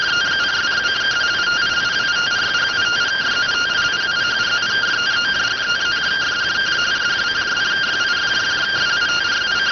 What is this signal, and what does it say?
Two-tone teleprinter signal, about 170 Hz shift.